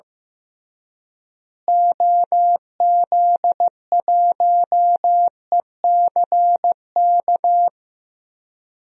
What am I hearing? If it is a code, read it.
Morse code, heard as a keyed tone, about 16 wpm, 700 Hz. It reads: OZ1ECK
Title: DXCC-100 Call Signs - 6 Characters - Repeated 3x - 15wpm